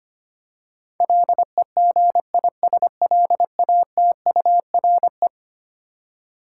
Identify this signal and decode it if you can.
Morse code, heard as a keyed tone, about 25 wpm, 700 Hz. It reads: LEGISLATURE
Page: morse-code-ninja-practice-25wpm